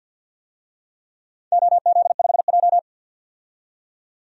Morse code, heard as a keyed tone, about 52 wpm, 700 Hz. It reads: OZ5J
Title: DXCC-100 Call Signs - 4 Characters - 50wpm